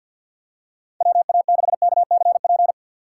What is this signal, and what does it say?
Morse code, heard as a keyed tone, about 53 wpm, 700 Hz. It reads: WA6KCP
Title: USA Call Signs - 6 Characters - 6x Character Spacing - 50wpm